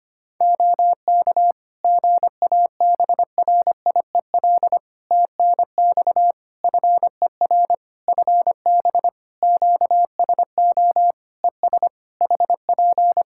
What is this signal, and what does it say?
Morse code, heard as a keyed tone, about 25 wpm, 700 Hz. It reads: OK GABRIEL TNX FER FB QSO ES HP
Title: Practice QSO - Exchange 4 - Ending - 25wpm